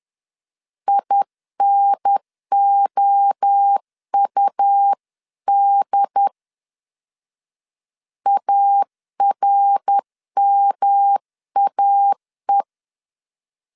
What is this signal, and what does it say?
Morse code, heard as a keyed tone, about 10 wpm, 800 Hz. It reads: INOUD ARMAE
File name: 10wpm